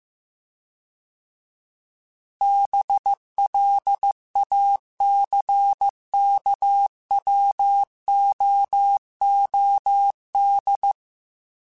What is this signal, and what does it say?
Morse code, heard as a keyed tone, about 15 wpm, 800 Hz. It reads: BLACKWOOD
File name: fromSMs / FromSMs mp3s / Cities@13